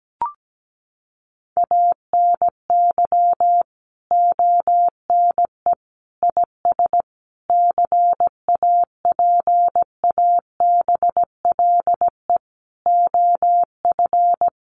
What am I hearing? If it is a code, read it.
Morse code, heard as a keyed tone, about 17 wpm, 700 Hz. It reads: ANY ONE IS CAPABLE OF
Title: Sentences from Top 1200 Words 17wpm